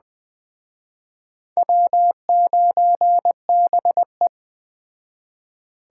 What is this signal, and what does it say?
Morse code, heard as a keyed tone, about 21 wpm, 700 Hz. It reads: W9BE
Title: Speed Racing - USA 1x2 Call Signs 17wpm